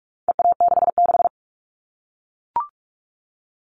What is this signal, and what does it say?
Morse code, heard as a keyed tone, about 44 wpm, 700 Hz. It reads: EA66
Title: All Cut Numbers - 4-Digit 45wpm